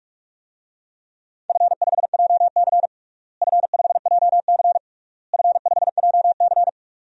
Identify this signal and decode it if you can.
Morse code, heard as a keyed tone, about 41 wpm, 700 Hz. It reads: F5JC F5JC F5JC
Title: DXCC-100 Call Signs - 4 Characters - Repeated 3x - 45wpm